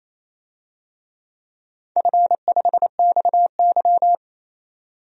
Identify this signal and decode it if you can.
Morse code, heard as a keyed tone, about 27 wpm, 700 Hz. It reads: F5XY
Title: DXCC-100 Call Signs - 4 Characters - 28wpm